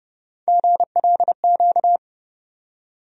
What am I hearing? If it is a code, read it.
Morse code, heard as a keyed tone, about 32 wpm, 700 Hz. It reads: GLQ
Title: USA Call Sign Postfixes - 3 Letters 30wpm